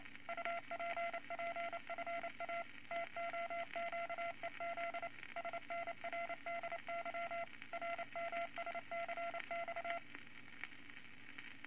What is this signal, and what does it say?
Morse code, heard as a keyed tone, about 28 wpm, 680 Hz, with noise. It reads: UPPFA TOQEZ SNRDY RMSCX
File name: Morse Code Records / 22 A LCM 172 Twenty Words Per Minute